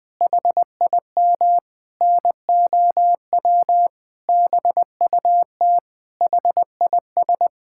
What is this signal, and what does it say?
Morse code, heard as a keyed tone, about 21 wpm, 700 Hz. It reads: HIM NOW BUT HIS
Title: Sentences from Top 1000 Words - Review Part 6 - 20wpm